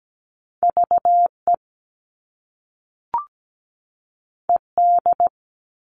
Morse code, heard as a keyed tone, about 17 wpm, 700 Hz. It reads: VE ED
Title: All Cut Numbers - 2-Digit 17wpm